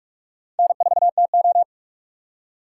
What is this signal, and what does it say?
Morse code, heard as a keyed tone, about 46 wpm, 700 Hz. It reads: N4TO
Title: USA Call Signs - 4 Characters - 6x Character Spacing - 45wpm